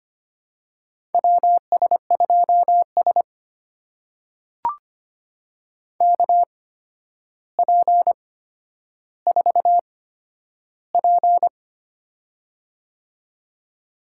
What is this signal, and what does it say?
Morse code, heard as a keyed tone, about 25 wpm, spaced as for 11 wpm, 700 Hz. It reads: WS2S KP4P
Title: USA Call Signs - 4 Characters - 8x Character Spacing - 25wpm